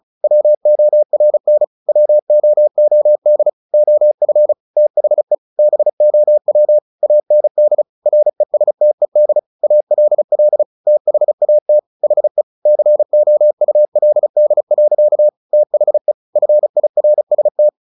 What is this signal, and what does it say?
Morse code, heard as a keyed tone, about 35 wpm, 600 Hz. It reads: WORN WOOD OF THE BOW AND RESTED ALL THAT HE COULD. THE FIRST
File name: the old man and the sea 2 35wpm